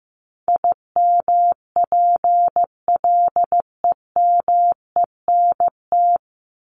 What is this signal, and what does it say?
Morse code, heard as a keyed tone, about 15 wpm, 700 Hz. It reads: IMPLEMENT
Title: Top 1501-1750 Words 15wpm